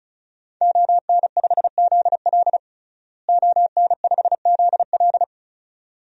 Morse code, heard as a keyed tone, about 36 wpm, 700 Hz. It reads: ON5ZL ON5ZL
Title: DXCC-100 Call Signs - 5 Characters - Repeated 2x - 35wpm